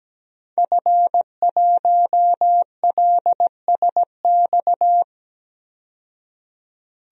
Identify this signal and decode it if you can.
Morse code, heard as a keyed tone, about 17 wpm, 700 Hz. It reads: F1LSX
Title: DXCC-100 Call Signs - 5 Characters - 17wpm